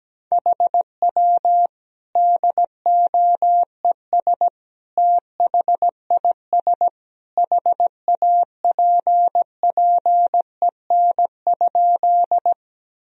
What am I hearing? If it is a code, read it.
Morse code, heard as a keyed tone, about 17 wpm, 700 Hz. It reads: HW DOES THIS HAPPEN?
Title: Sentences from Top 400 Words and Abbreviations 17wpm